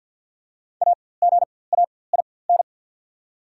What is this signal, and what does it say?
Morse code, heard as a keyed tone, about 48 wpm, spaced as for 23 wpm, 690 Hz. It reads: AGAIN
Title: 4x Character Spacing - Top 500 Words - 50wpm